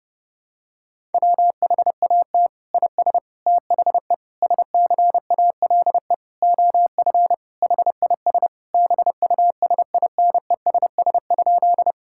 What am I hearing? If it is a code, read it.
Morse code, heard as a keyed tone, about 30 wpm, 700 Hz. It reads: WHAT IS THE SCALE OF HIS BUSINESS?
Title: Sentences from Top 1000 Words - Review Part 5 - 30wpm